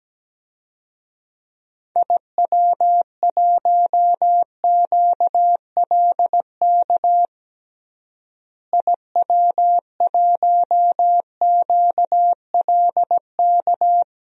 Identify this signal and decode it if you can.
Morse code, heard as a keyed tone, about 17 wpm, 700 Hz. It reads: IW1QLK IW1QLK
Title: DXCC-100 Call Signs - 6 Characters - Repeated 2x - 17wpm